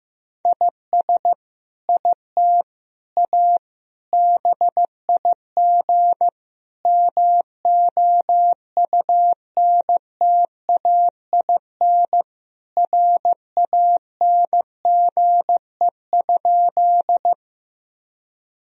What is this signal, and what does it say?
Morse code, heard as a keyed tone, about 16 wpm, 700 Hz. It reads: IS IT A BIG MOUNTAIN RANGE?
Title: Sentences from Top 1000 Words - Review Part 4 - 15wpm